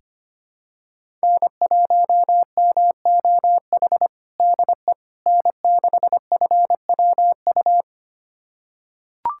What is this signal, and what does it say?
Morse code, heard as a keyed tone, about 25 wpm, 700 Hz. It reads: N1MOH DE N6FWU
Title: USA Callsign DE Callsign - 5 Characters - 25wpm